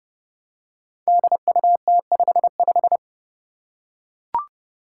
Morse code, heard as a keyed tone, about 30 wpm, 700 Hz. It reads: DUT55
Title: Five Challenging Letter-Number Combinations 30wpm